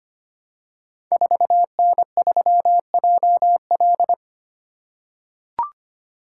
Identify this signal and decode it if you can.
Morse code, heard as a keyed tone, about 25 wpm, 700 Hz. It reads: 4N3JL
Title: Code Groups - 5 Letter:Numbers - 25wpm